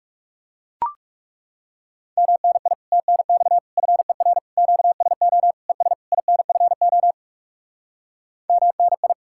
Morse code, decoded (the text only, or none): MNI TNX FER QSO ES INFO MNI